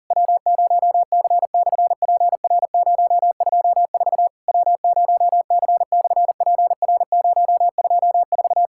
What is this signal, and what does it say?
Morse code, heard as a keyed tone, about 41 wpm, 700 Hz. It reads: W0C/PR024 W0C/PR024
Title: SOTA QSO Practice 40wpm